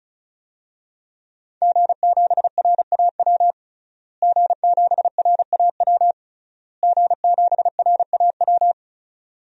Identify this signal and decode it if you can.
Morse code, heard as a keyed tone, about 36 wpm, 700 Hz. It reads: G7RAW G7RAW G7RAW
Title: DXCC-100 Call Signs - 5 Characters - Repeated 3x - 35wpm